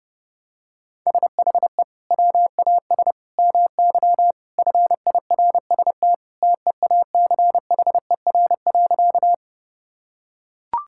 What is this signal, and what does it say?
Morse code, heard as a keyed tone, about 28 wpm, 700 Hz. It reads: SHE WAS MY FIRST TEACHER.
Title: Sentences from Top 1000 Words - Review Part 6 - 30wpm